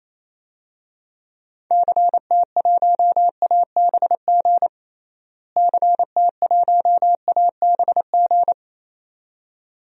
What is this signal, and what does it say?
Morse code, heard as a keyed tone, about 27 wpm, 700 Hz. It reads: CT1ABG CT1ABG
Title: DXCC-100 Call Signs - 6 Characters - Repeated 2x - 28wpm